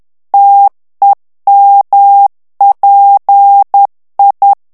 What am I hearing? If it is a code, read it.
Morse code, heard as a keyed tone, about 10 wpm, 800 Hz. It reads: TEMPI
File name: morse mp3 / ITALIEN texte clair 1 vitesse 10 mots minute